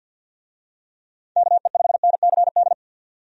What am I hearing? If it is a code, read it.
Morse code, heard as a keyed tone, about 46 wpm, 700 Hz. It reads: KE5NCD
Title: USA Call Signs - 6 Characters - 50wpm